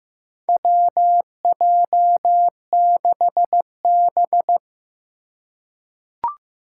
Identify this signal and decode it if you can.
Morse code, heard as a keyed tone, about 15 wpm, 700 Hz. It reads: WJ6B
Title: Four Challenging Letter-Number Combinations 15wpm